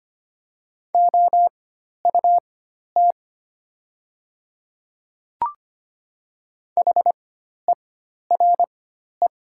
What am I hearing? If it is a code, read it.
Morse code, heard as a keyed tone, about 25 wpm, spaced as for 12 wpm, 700 Hz. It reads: OUT HERE
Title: 4x Character Spacing - Top 300 Words - 25wpm